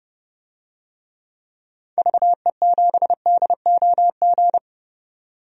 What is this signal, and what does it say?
Morse code, heard as a keyed tone, about 30 wpm, 700 Hz. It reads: VE7DOG
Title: DXCC-100 Call Signs - 6 Characters - 30wpm